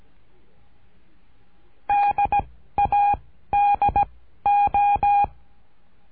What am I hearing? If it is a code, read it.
Morse code, heard as a keyed tone, about 17 wpm, 810 Hz. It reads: D A D O
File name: cursos / CURSO DE TELEGRAFIA / Telegrafia - Trilha 13